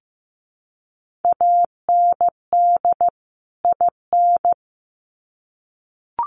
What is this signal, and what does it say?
Morse code, heard as a keyed tone, about 15 wpm, 700 Hz. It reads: AND IN
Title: Sets of 2 Words 15wpm